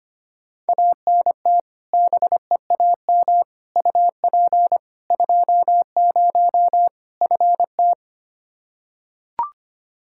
Morse code, heard as a keyed tone, about 25 wpm, 700 Hz. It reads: ANT BEAM UP 20 FT
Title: Practice QSO - Exchange 2 - Info - 25wpm